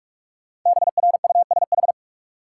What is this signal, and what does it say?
Morse code, heard as a keyed tone, about 46 wpm, 700 Hz. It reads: BRUSH
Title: Sentences from Top 1100 Words - New Word Drill Alt - 45wpm